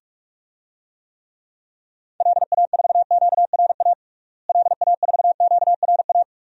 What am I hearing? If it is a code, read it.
Morse code, heard as a keyed tone, about 41 wpm, 700 Hz. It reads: LA4QRA LA4QRA
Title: DXCC-100 Call Signs - 6 Characters - Repeated 3x - 45wpm